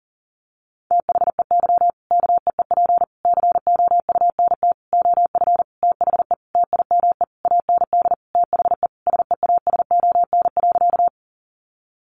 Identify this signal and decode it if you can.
Morse code, heard as a keyed tone, about 40 wpm, 700 Hz. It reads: THEY KEEP COUNT OF THE TIME AND THE SEASON.
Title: Sentences from Top 1500 Words 40wpm